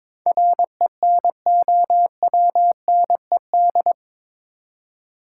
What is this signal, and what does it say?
Morse code, heard as a keyed tone, about 22 wpm, 700 Hz. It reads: RENOWNED
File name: practice / 22wpm / Eight Letter Words - 22wpm